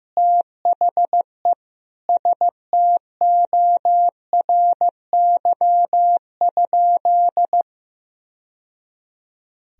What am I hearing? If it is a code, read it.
Morse code, heard as a keyed tone, about 15 wpm, 700 Hz. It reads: THE STORY?
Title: Sentences from Top 1000 Words - Review Part 1 - 15wpm